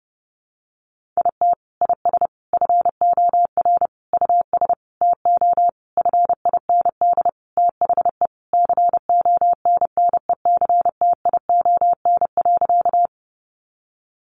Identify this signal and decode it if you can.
Morse code, heard as a keyed tone, about 30 wpm, 700 Hz. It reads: IT IS FOR US TO FIND THE CONNECTION.
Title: Sentences from Top 1300 Words 30wpm